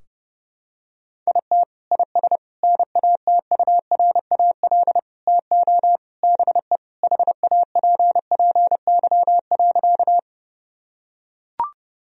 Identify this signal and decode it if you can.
Morse code, heard as a keyed tone, about 32 wpm, 700 Hz. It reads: IT IS NATURAL TO BE HAPPY.
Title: Sentences from Top 1000 Words - Review Part 4 - 30wpm